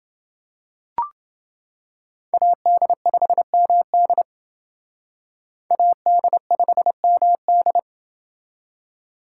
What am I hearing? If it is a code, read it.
Morse code, heard as a keyed tone, about 28 wpm, 700 Hz. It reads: AD5MD AD5MD